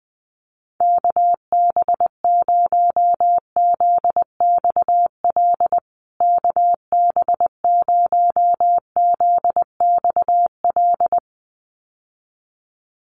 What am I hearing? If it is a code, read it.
Morse code, heard as a keyed tone, about 20 wpm, 700 Hz. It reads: KB0ZXL KB0ZXL
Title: USA Call Signs Repeated 2x 20wpm